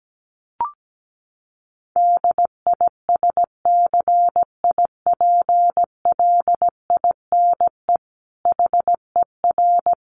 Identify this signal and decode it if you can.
Morse code, heard as a keyed tone, about 17 wpm, 700 Hz. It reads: DISCIPLINE HER
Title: Sentences from Top 900 Words 17wpm